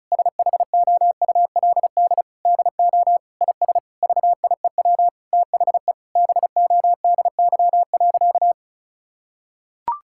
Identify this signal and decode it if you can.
Morse code, heard as a keyed tone, about 36 wpm, 700 Hz. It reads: SHOULD DO IS VIEW THE BODY.
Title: Sentences from Top 1000 Words - Review Part 6 - 35wpm